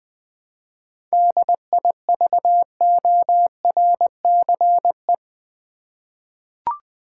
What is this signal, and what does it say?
Morse code, heard as a keyed tone, about 20 wpm, 700 Hz. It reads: DIVORCE
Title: 7 Letter Words - First 3 Letters Given - 20wpm